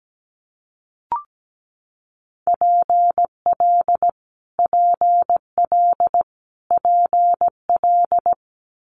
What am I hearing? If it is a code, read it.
Morse code, heard as a keyed tone, about 17 wpm, 700 Hz. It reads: PL PL PL
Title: Multi-letter Phonemes Repeated 3x 17wpm